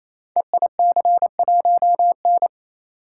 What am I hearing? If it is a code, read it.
Morse code, heard as a keyed tone, about 27 wpm, 700 Hz. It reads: EIC1N
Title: Code Groups - 5 Letter:Numbers - 28wpm